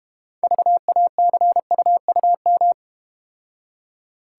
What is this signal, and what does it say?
Morse code, heard as a keyed tone, about 32 wpm, 700 Hz. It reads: VACUUM